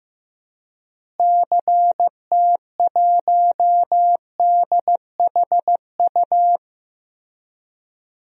Morse code, heard as a keyed tone, about 15 wpm, 700 Hz. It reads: CT1DHU
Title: DXCC-100 Call Signs - 6 Characters - Repeated 3x - 15wpm